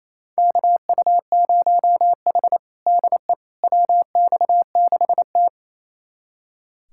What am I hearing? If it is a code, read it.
Morse code, heard as a keyed tone, about 28 wpm, 700 Hz. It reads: KU0H DE WX6T
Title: USA Callsign DE Callsign - 4 Characters - 28wpm